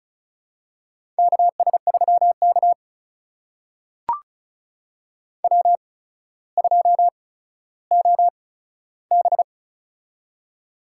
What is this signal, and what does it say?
Morse code, heard as a keyed tone, about 36 wpm, spaced as for 15 wpm, 700 Hz. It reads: KS3K W2OB
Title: USA Call Signs - 4 Characters - 8x Character Spacing - 35wpm